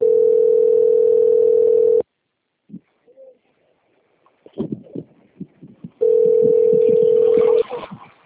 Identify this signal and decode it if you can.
Ringback tone (North America), 440 + 480 Hz.
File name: out- - - - - .37065